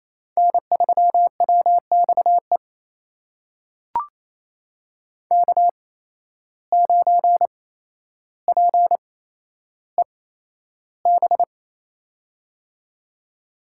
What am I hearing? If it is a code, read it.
Morse code, heard as a keyed tone, about 28 wpm, spaced as for 12 wpm, 700 Hz. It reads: N3WXE K9PEB
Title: USA Call Signs - 5 Characters - 8x Character Spacing - 28wpm